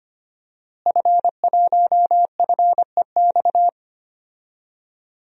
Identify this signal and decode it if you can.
Morse code, heard as a keyed tone, about 25 wpm, 700 Hz. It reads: F1FEX
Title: DXCC-100 Call Signs - 5 Characters - Repeated 2x - 25wpm